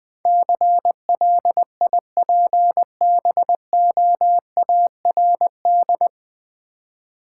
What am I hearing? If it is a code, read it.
Morse code, heard as a keyed tone, about 20 wpm, 700 Hz. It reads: CLIPBOARD
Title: Easy Words with Hints - 20wpm